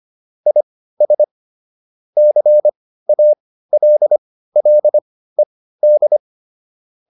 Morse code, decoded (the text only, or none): IS CALLED